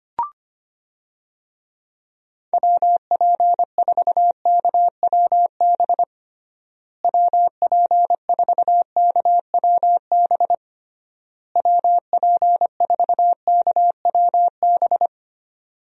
Morse code, heard as a keyed tone, about 25 wpm, 700 Hz. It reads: WP4KWB WP4KWB WP4KWB
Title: USA Call Signs - 6 Characters - Repeated 3x - 25wpm